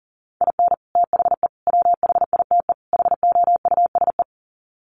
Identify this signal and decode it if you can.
Morse code, heard as a keyed tone, about 40 wpm, 700 Hz. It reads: IN THE WHITE HOUSE
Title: Sets of 4 Words 40wpm